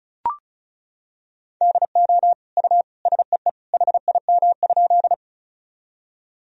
Morse code, heard as a keyed tone, about 34 wpm, 700 Hz. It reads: DO U SEE HIM?
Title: Sentences from Top 100 Words and Abbreviations 35wpm